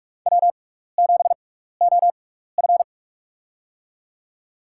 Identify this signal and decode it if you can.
Morse code, heard as a keyed tone, about 46 wpm, spaced as for 15 wpm, 700 Hz. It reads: W7OF
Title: USA Call Signs - 4 Characters - 6x Character Spacing - 45wpm